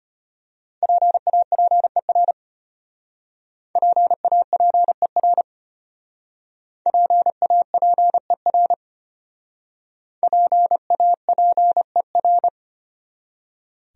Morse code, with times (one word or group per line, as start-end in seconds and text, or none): 0.8-2.3 PAPER
3.8-5.4 PAPER
6.9-8.7 PAPER
10.2-12.5 PAPER